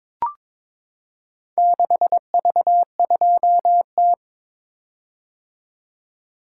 Morse code, heard as a keyed tone, about 22 wpm, 700 Hz. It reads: 6V2T
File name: practice / 22wpm / Four Letter-Number Combinations - 22wpm